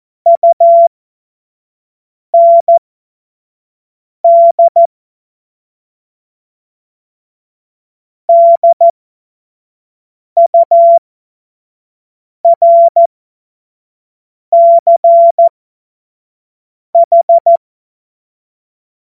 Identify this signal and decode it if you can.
Morse code, heard as a keyed tone, about 14 wpm, spaced as for 5 wpm, 680 Hz. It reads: UND DURCH
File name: Morsekurs / Witze / Witz 01. Juni 2021, 25 F70